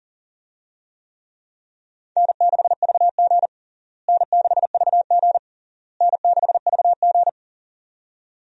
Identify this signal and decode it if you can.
Morse code, heard as a keyed tone, about 40 wpm, 700 Hz. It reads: N6VG N6VG N6VG
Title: USA Call Signs - 4 Characters - Repeated 3x - 40wpm